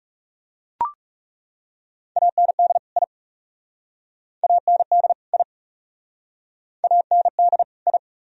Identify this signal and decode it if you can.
Morse code, heard as a keyed tone, about 40 wpm, 700 Hz. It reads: AND I AND I AND I